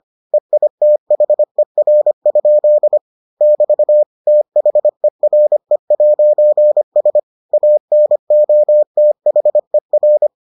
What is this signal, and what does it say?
Morse code, heard as a keyed tone, about 25 wpm, 600 Hz. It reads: EITHER? <BT> THERE'S ANOTHER